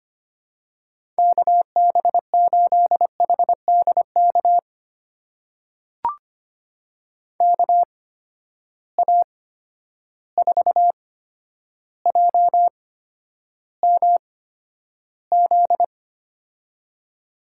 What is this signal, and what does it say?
Morse code, heard as a keyed tone, about 25 wpm, spaced as for 11 wpm, 700 Hz. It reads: KB8HDK KA4JMZ
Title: USA Call Signs - 6 Characters - 8x Character Spacing - 25wpm